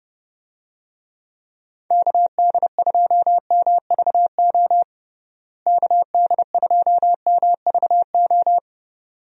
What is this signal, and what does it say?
Morse code, heard as a keyed tone, about 30 wpm, 700 Hz. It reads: KD2MVO KD2MVO
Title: USA Call Signs - 6 Characters - Repeated 2x - 30wpm